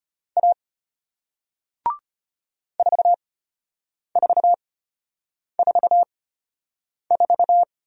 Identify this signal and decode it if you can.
Morse code, heard as a keyed tone, about 33 wpm, spaced as for 8 wpm, 700 Hz. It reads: A 4444